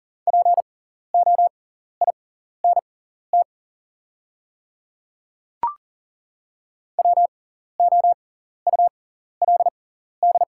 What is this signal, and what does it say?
Morse code, heard as a keyed tone, about 40 wpm, spaced as for 14 wpm, 690 Hz. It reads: POINT WOULD
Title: 6x Character Spacing - Top 200 Words - 40wpm